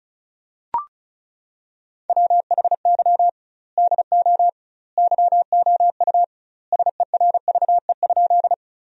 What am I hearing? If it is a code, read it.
Morse code, heard as a keyed tone, about 36 wpm, 700 Hz. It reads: WHY DO YOU SERVE?
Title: 2x Word-Spacing - Sentences from Top 400 Words - 35wpm